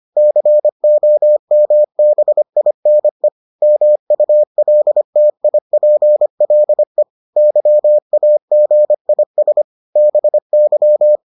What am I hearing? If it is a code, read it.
Morse code, heard as a keyed tone, about 25 wpm, 600 Hz. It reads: COMBINE MULTIPLE YAGIS BY